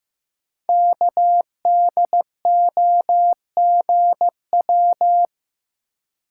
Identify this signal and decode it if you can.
Morse code, heard as a keyed tone, about 15 wpm, 700 Hz. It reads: KDOGW
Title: Code Groups - 5 Letters - 15wpm